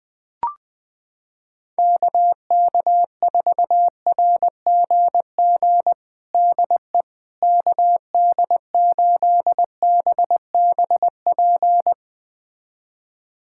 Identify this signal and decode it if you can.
Morse code, heard as a keyed tone, about 20 wpm, 700 Hz. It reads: KK4RGG DE KD8BBP
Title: USA Callsign DE Callsign 20wpm